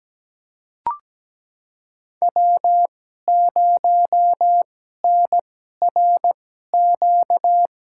Morse code, heard as a keyed tone, about 17 wpm, spaced as for 12 wpm, 700 Hz. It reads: W0NRQ
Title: USA Call Signs - 5 Characters - 2x Character Spacing - 17wpm